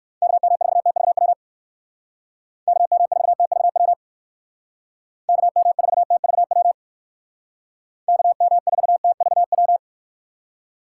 Morse code, heard as a keyed tone, about 58 wpm, 700 Hz. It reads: KM4TVW KM4TVW KM4TVW KM4TVW